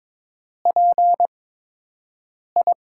Morse code, heard as a keyed tone, about 22 wpm, spaced as for 6 wpm, 700 Hz. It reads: PI
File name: Top 90 QSO Elements - 8x Character Spacing - 22wpm